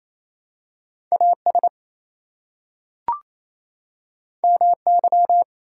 Morse code, heard as a keyed tone, about 29 wpm, 700 Hz. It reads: AS MY
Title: Two Letter Words - 28wpm